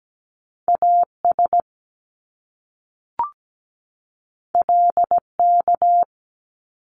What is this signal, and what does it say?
Morse code, heard as a keyed tone, about 17 wpm, 700 Hz. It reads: AS LK
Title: Top Two Letter Combinations in Words 17wpm